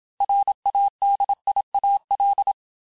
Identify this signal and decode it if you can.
Morse code, heard as a keyed tone, about 27 wpm, 800 Hz. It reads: RADIAL
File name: cours-du-mercredi / audios / cours-869-texte-LES-MOTS-vitesse-25-MPM